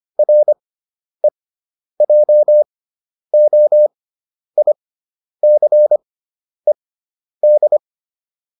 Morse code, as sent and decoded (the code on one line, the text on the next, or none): .-. .--- --- .. -.-. -..
REJOICED